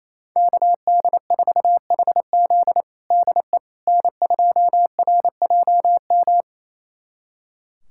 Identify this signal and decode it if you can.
Morse code, heard as a keyed tone, about 28 wpm, 700 Hz. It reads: KD4HZ DE N2RJM
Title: USA Callsign DE Callsign - 5 Characters - 28wpm